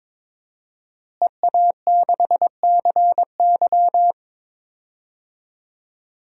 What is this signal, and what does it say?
Morse code, heard as a keyed tone, about 22 wpm, 700 Hz. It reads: EA6CY
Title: DXCC-100 Call Signs - 5 Characters - Repeated 2x - 22wpm